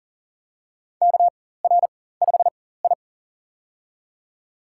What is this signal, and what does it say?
Morse code, heard as a keyed tone, about 40 wpm, spaced as for 19 wpm, 700 Hz. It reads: KR5I
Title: USA Call Signs - 4x Character Spacing - 40wpm